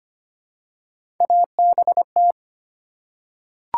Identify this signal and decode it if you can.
Morse code, heard as a keyed tone, about 25 wpm, 700 Hz. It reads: ABT
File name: Top 90 QSO Elements - 8x Character Spacing - 25wpm